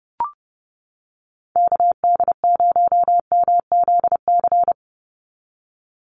Morse code, heard as a keyed tone, about 30 wpm, 700 Hz. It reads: KD0MZC
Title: USA Call Signs 30wpm